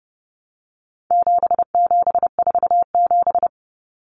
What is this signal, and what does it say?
Morse code, heard as a keyed tone, about 30 wpm, 700 Hz. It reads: 7747
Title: Common Cut Numbers - 4-Digit 30wpm